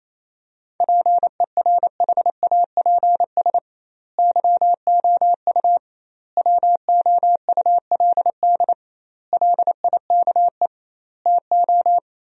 Morse code, heard as a keyed tone, about 28 wpm, 700 Hz. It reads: PERHAPS YOU WOULD LIKE TO
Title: 2x Word-Spacing - Sentences from Top 500 Words - 28wpm